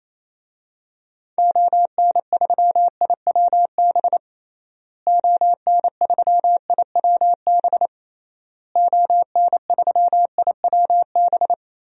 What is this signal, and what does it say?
Morse code, heard as a keyed tone, about 28 wpm, 700 Hz. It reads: ON3IWB ON3IWB ON3IWB
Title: DXCC-100 Call Signs - 6 Characters - Repeated 3x - 28wpm